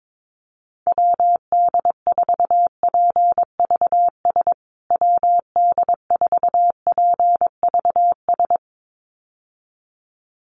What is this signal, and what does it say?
Morse code, heard as a keyed tone, about 22 wpm, 700 Hz. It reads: WD4PVS WD4PVS
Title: USA Call Signs Repeated 2x 22wpm